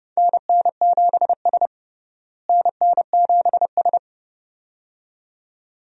Morse code, heard as a keyed tone, about 32 wpm, 700 Hz. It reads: NN7S NN7S
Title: USA Call Signs - 4 Characters - Repeated 2x - 30wpm